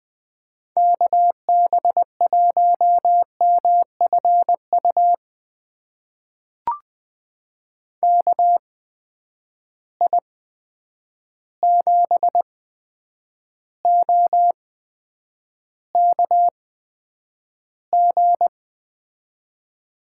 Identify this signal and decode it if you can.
Morse code, heard as a keyed tone, about 21 wpm, spaced as for 9 wpm, 700 Hz. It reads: KB1MFU KI7OKG
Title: USA Call Signs - 6 Characters - 8x Character Spacing - 20wpm